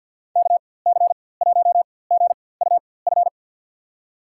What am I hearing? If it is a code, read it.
Morse code, heard as a keyed tone, about 52 wpm, spaced as for 23 wpm, 700 Hz. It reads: KC1GUF
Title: USA Call Signs - 6 Characters - 4x Character Spacing - 50wpm